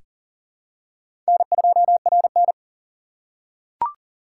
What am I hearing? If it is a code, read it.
Morse code, heard as a keyed tone, about 40 wpm, 700 Hz. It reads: N2RN
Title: USA Call Signs - 2x Character Spacing - 40wpm